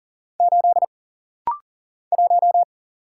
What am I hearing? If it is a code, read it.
Morse code, heard as a keyed tone, about 40 wpm, 700 Hz. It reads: 8 1
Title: Single Letter-Number Alt - Rapid-Fire 40wpm